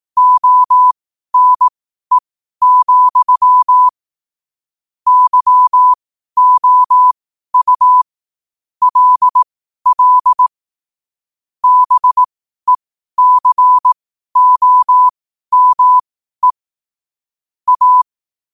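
Morse code, heard as a keyed tone, about 18 wpm, spaced as for 12 wpm, 1000 Hz. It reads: ONE, YOU LL BECOME A
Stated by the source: Morse code sent at 12 words per minute.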